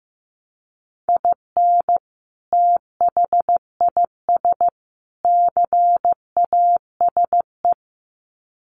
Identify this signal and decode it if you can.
Morse code, heard as a keyed tone, about 15 wpm, 700 Hz. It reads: IN THIS CASE
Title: Sets of 3 Words 15wpm